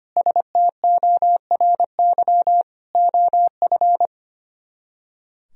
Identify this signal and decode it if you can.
Morse code, heard as a keyed tone, about 31 wpm, 700 Hz. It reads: STORY OF
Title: Speed-Racing - Sets of 3 Words 25wpm